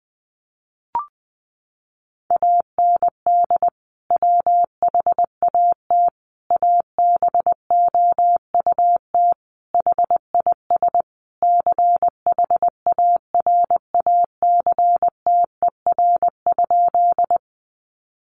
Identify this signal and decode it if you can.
Morse code, heard as a keyed tone, about 20 wpm, 700 Hz. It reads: AND WHAT ABOUT HIS CHARACTER?
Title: Sentences from Top 1200 Words 20wpm